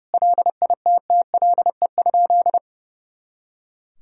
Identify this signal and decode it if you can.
Morse code, heard as a keyed tone, about 30 wpm, 700 Hz. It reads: LITTLE?
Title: Sentences from Top 500 Words and Abbreviations 30wpm